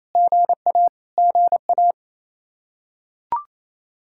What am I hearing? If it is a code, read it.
Morse code, heard as a keyed tone, about 28 wpm, 700 Hz. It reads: GA GA
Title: Practice QSO - Exchange 1 - Beginning - 28wpm